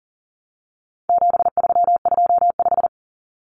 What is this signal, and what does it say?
Morse code, heard as a keyed tone, about 40 wpm, 700 Hz. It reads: 7325
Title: Numbers - 4-Digit 40wpm